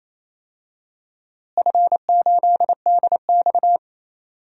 Morse code, heard as a keyed tone, about 27 wpm, 700 Hz. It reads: F8DX
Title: DXCC-100 Call Signs - 4 Characters - Repeated 2x - 28wpm